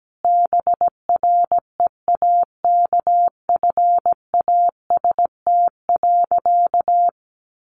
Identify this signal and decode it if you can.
Morse code, heard as a keyed tone, about 17 wpm, 700 Hz. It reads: BREAKFAST.
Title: Sentences from Top 1500 Words 17wpm